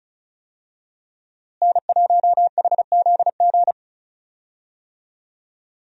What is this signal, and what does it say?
Morse code, heard as a keyed tone, about 36 wpm, 700 Hz. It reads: N1HZG
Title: USA Call Signs - 5 Characters - Repeated 3x - 35wpm